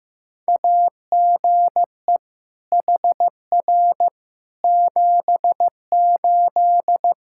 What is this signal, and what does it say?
Morse code, heard as a keyed tone, about 15 wpm, 700 Hz. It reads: AGE HR 78
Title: Practice QSO - Exchange 3 - More Info - 15wpm